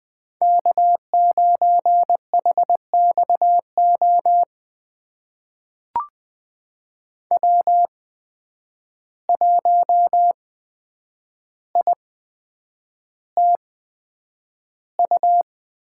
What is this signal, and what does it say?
Morse code, heard as a keyed tone, about 20 wpm, spaced as for 9 wpm, 700 Hz. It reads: K9HXO W1ITU
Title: USA Call Signs - 5 Characters - 8x Character Spacing - 20wpm